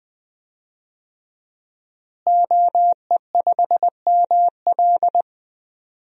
Morse code, heard as a keyed tone, about 21 wpm, 700 Hz. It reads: OE5ML
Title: DXCC-100 Call Signs - 5 Characters - Repeated 3x - 20wpm